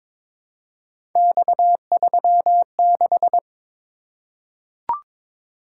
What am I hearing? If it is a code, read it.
Morse code, heard as a keyed tone, about 22 wpm, 700 Hz. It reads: X36
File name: practice / 22wpm / Three Letter-Number Combinations - 22wpm